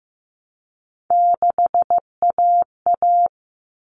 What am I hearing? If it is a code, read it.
Morse code, heard as a keyed tone, about 15 wpm, 700 Hz. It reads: 6AA
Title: All Cut Numbers - 3-Digit 15wpm